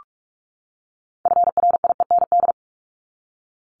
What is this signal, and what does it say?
Morse code, heard as a keyed tone, about 44 wpm, 700 Hz. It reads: FRIEND
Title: Top 500 Words 45wpm